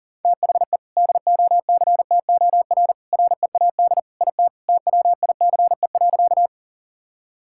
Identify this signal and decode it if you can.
Morse code, heard as a keyed tone, about 40 wpm, 700 Hz. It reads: THE DOCTOR READ IT TWICE.
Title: Sentences from Top 1600 Words 40wpm